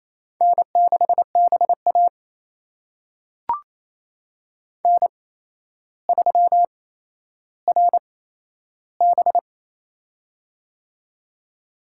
Morse code, heard as a keyed tone, about 28 wpm, spaced as for 12 wpm, 700 Hz. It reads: N6BA N3RB
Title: USA Call Signs - 4 Characters - 8x Character Spacing - 28wpm